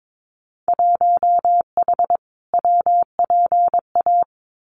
Morse code, heard as a keyed tone, about 22 wpm, 700 Hz. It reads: 1H WPA
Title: Winter Field Day Exchanges 22wpm